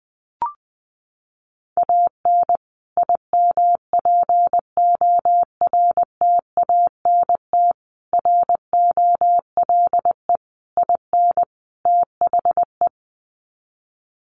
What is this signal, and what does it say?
Morse code, heard as a keyed tone, about 20 wpm, 700 Hz. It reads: AN IMPORTANT ROLE IN THE
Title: Sets of 5 Words Encore2 20wpm